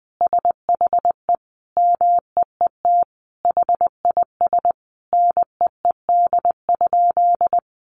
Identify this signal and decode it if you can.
Morse code, heard as a keyed tone, about 20 wpm, 700 Hz. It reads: SHE MEET HIS NEED?
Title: Sentences from Top 1100 Words 20wpm